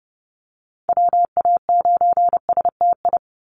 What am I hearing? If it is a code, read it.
Morse code, heard as a keyed tone, about 30 wpm, 700 Hz. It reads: WA9STI
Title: NA SOTA Activator Callsigns 30wpm